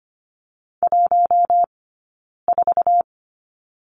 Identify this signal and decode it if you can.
Morse code, heard as a keyed tone, about 25 wpm, 700 Hz. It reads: 1 4
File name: course / 25:10wpm / 092 Lesson 16d - Letter c - Two Characters 25wpm